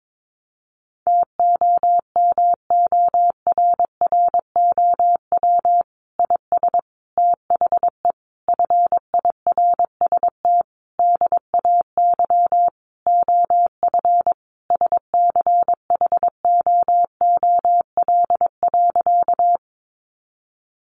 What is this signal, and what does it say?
Morse code, heard as a keyed tone, about 22 wpm, 700 Hz. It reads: TOMORROW IS THE FIRST DAY OF SCHOOL.
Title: Additional Easy Phrases 22wpm